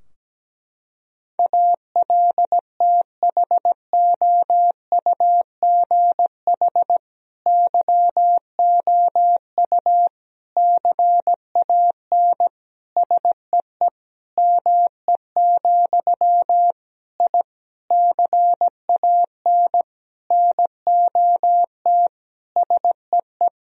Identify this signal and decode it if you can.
Morse code, heard as a keyed tone, about 17 wpm, 700 Hz. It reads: ALTHOUGH YOU CAN SEE ME, I CAN NOT SEE
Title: Sentences from Top 1000 Words - Review Part 3 - 17wpm